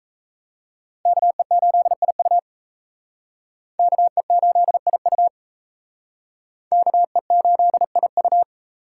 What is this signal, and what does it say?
Morse code, heard as a keyed tone, about 36 wpm, 700 Hz. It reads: KE8IU KE8IU KE8IU